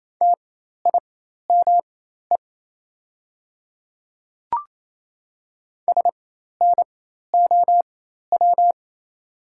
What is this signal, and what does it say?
Morse code, heard as a keyed tone, about 28 wpm, spaced as for 13 wpm, 700 Hz. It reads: TIME SNOW
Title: 4x Character Spacing - Top 500 Words - 28wpm